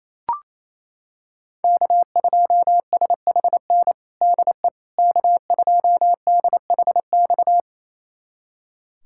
Morse code, heard as a keyed tone, about 28 wpm, 700 Hz. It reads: K2SHN DE K2DHX
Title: USA Callsign DE Callsign - 5 Characters - 28wpm